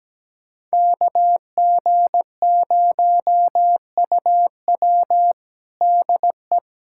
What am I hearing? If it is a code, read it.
Morse code, heard as a keyed tone, about 17 wpm, 700 Hz. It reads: KG0UW DE
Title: USA Callsign DE Callsign - 5 Characters - 17wpm